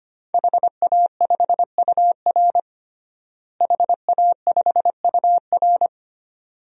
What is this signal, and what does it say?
Morse code, heard as a keyed tone, about 25 wpm, 700 Hz. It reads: HA5UR HA5UR
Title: DXCC-100 Call Signs - 5 Characters - Repeated 2x - 25wpm